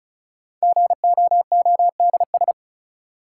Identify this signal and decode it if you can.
Morse code, heard as a keyed tone, about 36 wpm, 700 Hz. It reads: GOODS
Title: Sentences from Top 1000 Words - Review Part 6 - New Word Drill - 35wpm